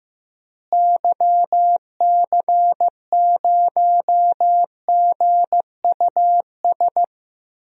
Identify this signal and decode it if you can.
Morse code, heard as a keyed tone, about 16 wpm, 700 Hz. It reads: YC0GUS
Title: DXCC-100 Call Signs - 6 Characters - Repeated 3x - 15wpm